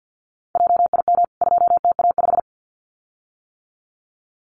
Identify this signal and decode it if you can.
Morse code, heard as a keyed tone, about 49 wpm, 700 Hz. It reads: JIM 2TA5
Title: CWT Exchange 50wpm